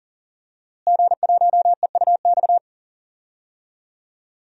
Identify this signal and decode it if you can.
Morse code, heard as a keyed tone, about 40 wpm, 700 Hz. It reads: G1EUX
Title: DXCC-100 Call Signs - 5 Characters - Repeated 2x - 40wpm